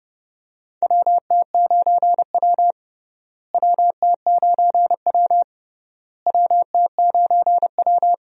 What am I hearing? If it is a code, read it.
Morse code, heard as a keyed tone, about 29 wpm, 700 Hz. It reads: WT9W WT9W WT9W
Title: USA Call Signs - 4 Characters - Repeated 3x - 30wpm